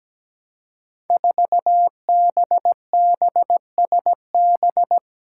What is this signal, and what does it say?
Morse code, heard as a keyed tone, about 17 wpm, 700 Hz. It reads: 4BBSB
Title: Five Challenging Letter-Number Combinations 17wpm